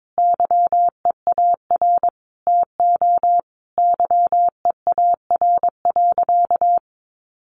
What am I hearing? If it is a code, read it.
Morse code, heard as a keyed tone, about 22 wpm, 700 Hz. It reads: YEAR TO YEAR.
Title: Sentences from Top 1200 Words 22wpm